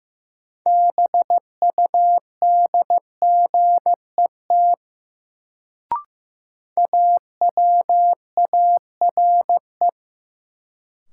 Morse code, heard as a keyed tone, about 15 wpm, 700 Hz. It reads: BUDGET AWARE
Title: Sentences from Top 1200 Words - New Word Drill - 15wpm